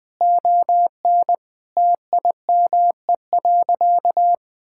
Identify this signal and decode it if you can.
Morse code, heard as a keyed tone, about 20 wpm, 700 Hz. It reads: ON TIME.
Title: Sentences with 3-Word Phrases - Spoken Beginning - 20wpm